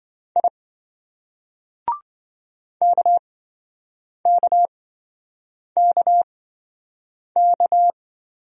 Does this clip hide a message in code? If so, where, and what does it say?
Yes: Morse code from 0.4 to 7.9 s, heard as a keyed tone, about 27 wpm, spaced as for 7 wpm, 700 Hz. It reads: I KKKK